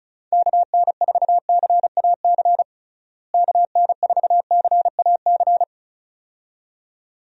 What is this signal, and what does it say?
Morse code, heard as a keyed tone, about 34 wpm, 700 Hz. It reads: KN4CAC KN4CAC
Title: USA Call Signs - 6 Characters - Repeated 2x - 35wpm